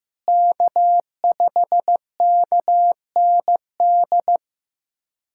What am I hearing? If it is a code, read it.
Morse code, heard as a keyed tone, about 15 wpm, 700 Hz. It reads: K5KND
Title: K1USN SST QSO Practice - 15wpm